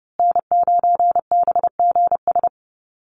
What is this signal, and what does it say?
Morse code, heard as a keyed tone, about 30 wpm, 700 Hz. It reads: N9BGS
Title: USA Call Signs Repeated 3x 30wpm